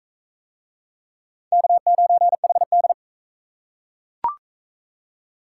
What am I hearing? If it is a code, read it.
Morse code, heard as a keyed tone, about 41 wpm, 700 Hz. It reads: K9HD